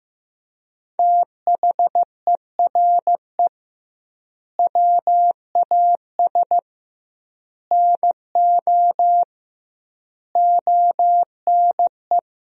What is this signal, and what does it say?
Morse code, heard as a keyed tone, about 15 wpm, 700 Hz. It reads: THERE WAS NO ONE
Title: 2x Word-Spacing - Sentences from Top 200 Words - 15wpm